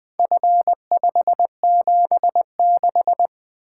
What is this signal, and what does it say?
Morse code, heard as a keyed tone, about 20 wpm, 700 Hz. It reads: F576
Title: Four Letter-Number Combinations - 20wpm